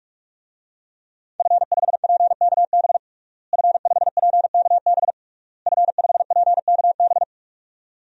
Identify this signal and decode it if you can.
Morse code, heard as a keyed tone, about 46 wpm, 700 Hz. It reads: F5PKB F5PKB F5PKB
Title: DXCC-100 Call Signs - 5 Characters - Repeated 3x - 45wpm